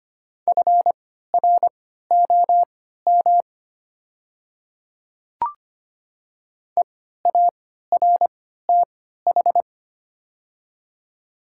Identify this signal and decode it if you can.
Morse code, heard as a keyed tone, about 25 wpm, spaced as for 14 wpm, 690 Hz. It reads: FROM EARTH
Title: 3x Character Spacing - Top 300 Words - 25wpm